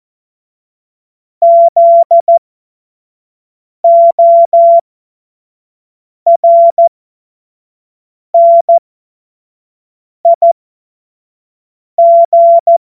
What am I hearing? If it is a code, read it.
Morse code, heard as a keyed tone, about 14 wpm, spaced as for 5 wpm, 680 Hz. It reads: ZORNIG
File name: Morsekurs / Witze / Witz 06 Februar 2024 25 F70